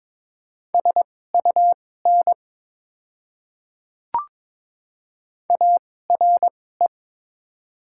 Morse code, heard as a keyed tone, about 22 wpm, spaced as for 16 wpm, 700 Hz. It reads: SUN ARE
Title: 2x Character Spacing - Top 500 Words - 22wpm